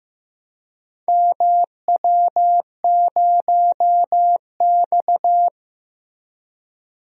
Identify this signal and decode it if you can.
Morse code, heard as a keyed tone, about 15 wpm, 700 Hz. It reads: MW0X
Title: DXCC-100 Call Signs - 4 Characters - Repeated 2x - 15wpm